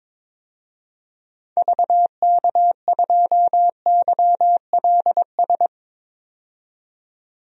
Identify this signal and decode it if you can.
Morse code, heard as a keyed tone, about 22 wpm, 700 Hz. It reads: VK2YLS
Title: DXCC-100 Call Signs - 6 Characters - 22wpm